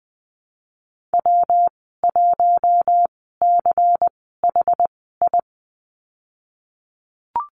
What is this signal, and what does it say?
Morse code, heard as a keyed tone, about 20 wpm, spaced as for 15 wpm, 700 Hz. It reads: W1CHI
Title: 167 Lesson 27g - Number 7 - Call Signs 20wpm